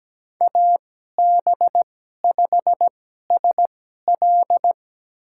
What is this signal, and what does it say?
Morse code, heard as a keyed tone, about 17 wpm, spaced as for 12 wpm, 700 Hz. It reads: AB5SL
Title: USA Call Signs - 5 Characters - 2x Character Spacing - 17wpm